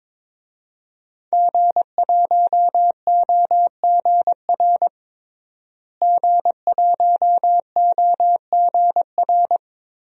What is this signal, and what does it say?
Morse code, heard as a keyed tone, about 22 wpm, 700 Hz. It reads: G1OGR G1OGR
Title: DXCC-100 Call Signs - 5 Characters - Repeated 2x - 22wpm